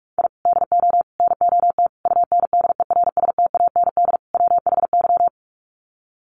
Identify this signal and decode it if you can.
Morse code, heard as a keyed tone, about 44 wpm, 700 Hz. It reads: I DO NOT UNDERSTAND WHY
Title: Sets of 5 Words Encore 45wpm